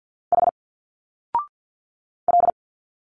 Morse code, heard as a keyed tone, about 49 wpm, 700 Hz. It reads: H L
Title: Single Letters 50wpm